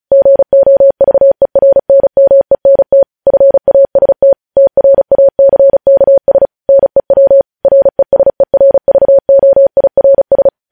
Morse code, heard as a keyed tone, about 36 wpm, 550 Hz. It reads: GOVERNMENT FAST TRACKS NEW RESERVOIRS
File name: podcast / News headlines-1x-35x35